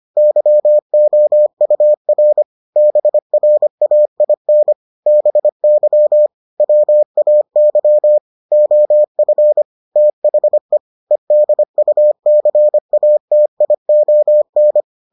Morse code, heard as a keyed tone, about 25 wpm, 600 Hz. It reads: YOUR BRAIN BY WAY OF THE EDUCATION